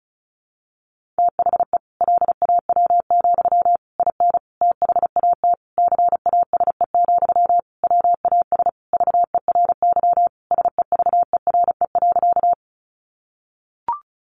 Morse code, heard as a keyed tone, about 34 wpm, 700 Hz. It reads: THE LAW, IN THAT CASE, WAS VERY SEVERE.
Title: Sentences from Top 1400 Words 35wpm